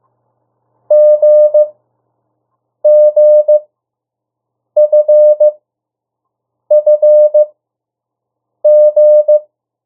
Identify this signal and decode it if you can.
Morse code, heard as a keyed tone, about 15 wpm, spaced as for 6 wpm, 600 Hz. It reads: GGFFG
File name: urok 1